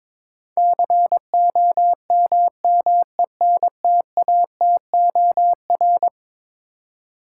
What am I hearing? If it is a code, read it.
Morse code, heard as a keyed tone, about 22 wpm, 700 Hz. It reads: COMMENTATOR
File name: practice / 22wpm / Eleven Letter Words - 22wpm